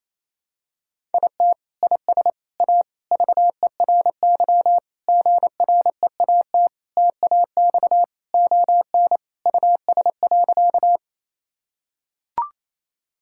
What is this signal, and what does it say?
Morse code, heard as a keyed tone, about 28 wpm, 700 Hz. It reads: IT IS A VERY GREAT TAX ON US.
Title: Sentences from Top 1000 Words - Review Part 5 - 28wpm